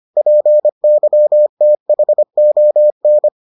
Morse code, heard as a keyed tone, about 25 wpm, 600 Hz. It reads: PYTHON